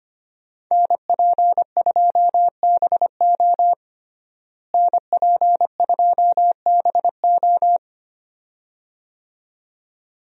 Morse code, heard as a keyed tone, about 25 wpm, 700 Hz. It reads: NP2BO NP2BO
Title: USA Call Signs - 5 Characters - Repeated 2x - 25wpm